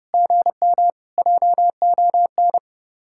Morse code, heard as a keyed tone, about 35 wpm, 700 Hz. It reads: GM JON
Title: Speed-Racing Common QSO Phrases - 30wpm